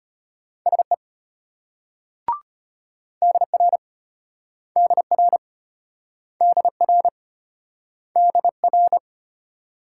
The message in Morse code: SE DR DR DR DR

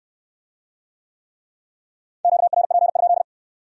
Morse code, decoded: QRP?

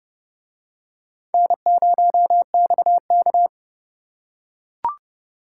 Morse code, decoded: N0XK

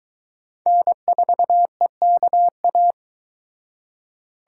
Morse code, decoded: N4EKA

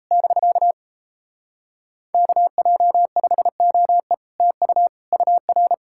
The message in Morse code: <BK> KJ5OE TU UR

